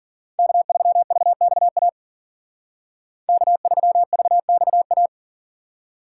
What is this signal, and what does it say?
Morse code, heard as a keyed tone, about 49 wpm, 700 Hz. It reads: K3VXA K3VXA
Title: Speed Racing - USA 1x3 Call Signs 40wpm